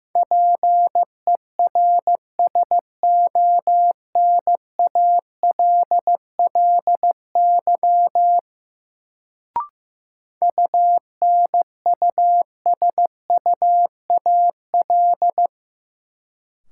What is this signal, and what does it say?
Morse code, heard as a keyed tone, about 15 wpm, 700 Hz. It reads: PERSONALLY UNUSUAL
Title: Sentences from Top 1400 Words - New Word Drill - 15wpm